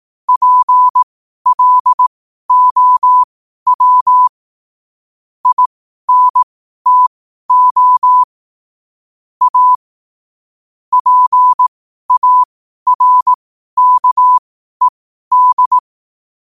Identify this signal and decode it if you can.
Morse code, heard as a keyed tone, about 18 wpm, spaced as for 12 wpm, 1000 Hz. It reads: PLOW INTO A PARKED
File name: qotdfiles / 12wpm